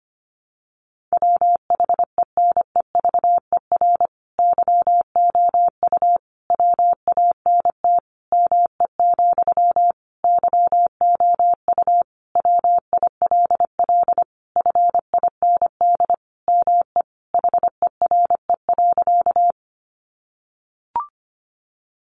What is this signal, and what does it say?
Morse code, heard as a keyed tone, about 25 wpm, 700 Hz. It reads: WHENEVER YOU WANT ME, YOU WILL FIND ME HERE.
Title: Sentences from Top 1200 Words 25wpm